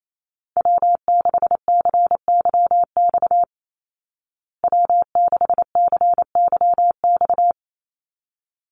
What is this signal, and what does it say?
Morse code, heard as a keyed tone, about 28 wpm, 690 Hz. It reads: W6CYX W6CYX
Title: Call Signs Repeated 3x - 28wpm